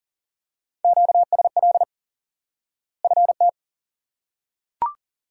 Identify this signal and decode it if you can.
Morse code, heard as a keyed tone, about 40 wpm, 700 Hz. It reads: QSL FT
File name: IQE Course 3 - Lesson 89 - YAGI - Two Elements - 40wpm